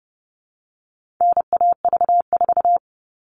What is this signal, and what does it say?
Morse code, heard as a keyed tone, about 30 wpm, 700 Hz. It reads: NAV4
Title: All Cut Numbers - 4-Digit 30wpm